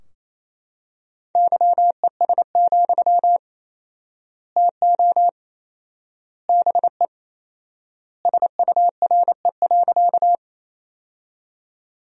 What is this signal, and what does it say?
Morse code, heard as a keyed tone, about 28 wpm, 700 Hz. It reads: YES, TO BE SURE.
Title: 4x Word-Spacing - Sentences from Top 500 Words - 28wpm